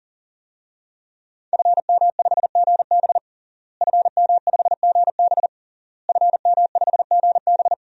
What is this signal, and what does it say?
Morse code, heard as a keyed tone, about 40 wpm, 700 Hz. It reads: FM5GB FM5GB FM5GB
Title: DXCC-100 Call Signs - 5 Characters - Repeated 3x - 40wpm